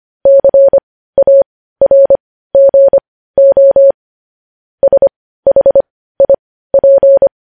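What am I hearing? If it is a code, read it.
Morse code, heard as a keyed tone, about 25 wpm, spaced as for 15 wpm, 550 Hz. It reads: CARGO SHIP
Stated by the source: Morse code news headlines at 15 WPM.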